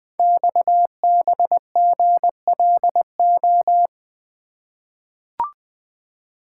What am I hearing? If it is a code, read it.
Morse code, heard as a keyed tone, about 20 wpm, 700 Hz. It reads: XBGLO
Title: Code Groups - 5 Letter:Numbers - 20wpm